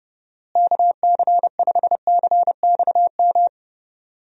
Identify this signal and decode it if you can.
Morse code, heard as a keyed tone, about 30 wpm, 700 Hz. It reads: KC5CXM
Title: USA Call Signs - 6 Characters - Repeated 3x - 30wpm